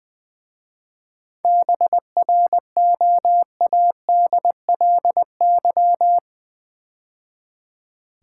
Morse code, heard as a keyed tone, about 20 wpm, 690 Hz. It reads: BROADLY
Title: 7 Letter Words - First 3 Letters Given - 20wpm